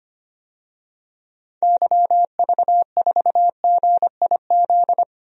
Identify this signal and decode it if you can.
Morse code, heard as a keyed tone, about 25 wpm, 700 Hz. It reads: YV4GIZ
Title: DXCC-100 Call Signs - 6 Characters - Repeated 2x - 25wpm